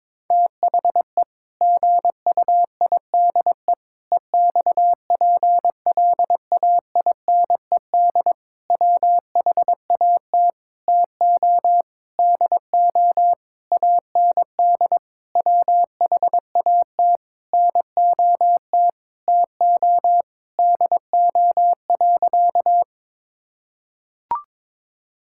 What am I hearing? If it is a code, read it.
Morse code, heard as a keyed tone, about 22 wpm, 700 Hz. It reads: THE GUIDE EXPLAINED WHAT TO DO AND WHAT NOT TO DO.
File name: practice / 22wpm / Sentences with 8-Word Phrases - 22wpm